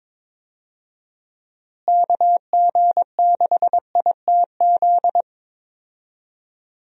Morse code, heard as a keyed tone, about 21 wpm, 700 Hz. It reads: KG6ITZ
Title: USA Call Signs - 6 Characters - Repeated 2x - 22wpm